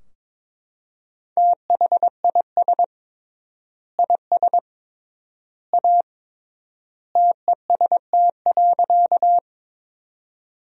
Morse code, heard as a keyed tone, about 22 wpm, 700 Hz. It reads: THIS IS A TEST.
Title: 3x Word-Spacing - Sentences from Top 500 Words - 22wpm